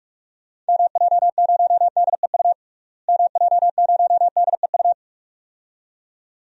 Morse code, heard as a keyed tone, about 46 wpm, 700 Hz. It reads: MJ0DEU MJ0DEU
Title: DXCC-100 Call Signs - 6 Characters - Repeated 2x - 45wpm